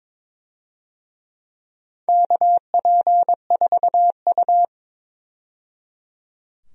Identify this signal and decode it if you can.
Morse code, heard as a keyed tone, about 22 wpm, 700 Hz. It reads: KP4U
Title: DXCC-100 Call Signs - 4 Characters - 22wpm